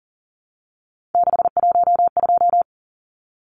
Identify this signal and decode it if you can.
Morse code, heard as a keyed tone, about 40 wpm, 700 Hz. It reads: 612
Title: Numbers - 3-Digit 40wpm